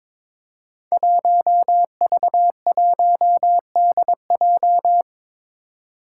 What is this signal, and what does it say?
Morse code, heard as a keyed tone, about 22 wpm, 700 Hz. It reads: 1V1DJ
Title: Five Challenging Letter-Number Combinations 22wpm